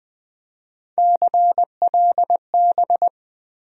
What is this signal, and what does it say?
Morse code, heard as a keyed tone, about 20 wpm, 700 Hz. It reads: CLB
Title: USA Call Sign Postfixes - 3 Letters 20wpm